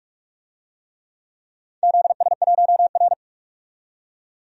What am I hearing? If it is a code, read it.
Morse code, heard as a keyed tone, about 46 wpm, 700 Hz. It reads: ZS1R
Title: DXCC-100 Call Signs - 4 Characters - 45wpm